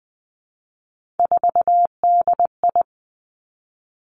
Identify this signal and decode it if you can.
Morse code, heard as a keyed tone, about 20 wpm, 700 Hz. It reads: 4DI
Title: Three Challenging Letter-Number Combinations 20wpm